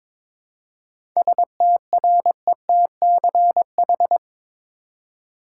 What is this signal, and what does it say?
Morse code, heard as a keyed tone, about 22 wpm, 700 Hz. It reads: STRETCH
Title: Seven Letter Words - 22wpm